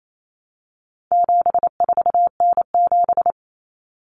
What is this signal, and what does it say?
Morse code, heard as a keyed tone, about 28 wpm, 700 Hz. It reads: 74N7
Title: Common Cut Numbers - 4-Digit 28wpm